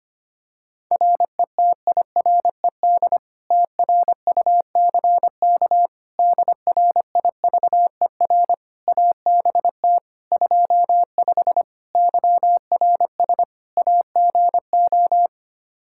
Morse code, heard as a keyed tone, about 25 wpm, 700 Hz. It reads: RETIRED TRUCK DRIVER ABT 25 YRS AGO
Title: Practice QSO - Exchange 3 - More Info - 25wpm